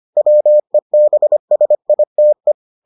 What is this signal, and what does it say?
Morse code, heard as a keyed tone, about 25 wpm, 600 Hz. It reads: WEBSITE